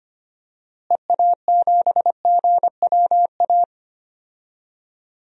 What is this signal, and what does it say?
Morse code, heard as a keyed tone, about 25 wpm, 700 Hz. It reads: EA7GWA
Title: DXCC-100 Call Signs - 6 Characters - Repeated 3x - 25wpm